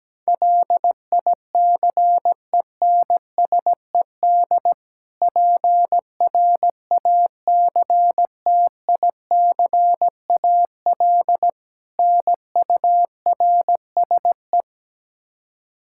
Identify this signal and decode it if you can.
Morse code, heard as a keyed tone, about 17 wpm, 700 Hz. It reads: LICENSED PRACTICAL NURSE
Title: Practice QSO - Exchange 3 - More Info - 17wpm